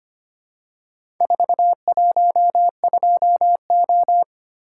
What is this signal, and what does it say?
Morse code, heard as a keyed tone, about 25 wpm, 700 Hz. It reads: 412O
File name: practice / 25wpm / Four Letter-Number Combinations - 25wpm